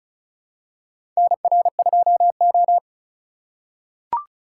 Morse code, heard as a keyed tone, about 36 wpm, 700 Hz. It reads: NR2O
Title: USA Call Signs - 4 Characters - 3x Character Spacing - 35wpm